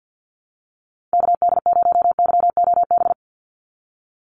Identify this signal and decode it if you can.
Morse code, heard as a keyed tone, about 50 wpm, 700 Hz. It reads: KD0YOB
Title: NA SOTA Activator Callsigns 50wpm